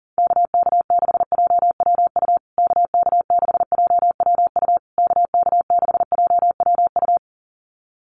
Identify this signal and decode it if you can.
Morse code, heard as a keyed tone, about 40 wpm, 700 Hz. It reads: KK6JWU KK6JWU KK6JWU
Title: USA Call Signs Repeated 3x 40wpm